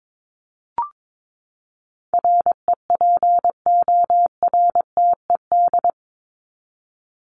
Morse code, heard as a keyed tone, about 22 wpm, 700 Hz. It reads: REPORTED
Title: ED Suffix - Top 100 Words with Base Words 22wpm